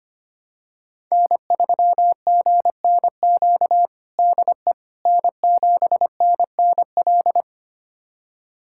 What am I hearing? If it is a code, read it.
Morse code, heard as a keyed tone, about 25 wpm, 700 Hz. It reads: N3GNQ DE N7NNL
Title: USA Callsign DE Callsign - 5 Characters - 25wpm